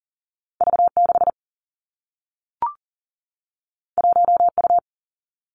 Morse code, heard as a keyed tone, about 39 wpm, 700 Hz. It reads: V 6 1 U
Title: Two Challenging Letter-Number Combinations 40wpm